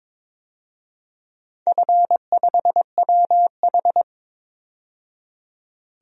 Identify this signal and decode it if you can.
Morse code, heard as a keyed tone, about 22 wpm, 700 Hz. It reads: F5WH
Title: DXCC-100 Call Signs - 4 Characters - 22wpm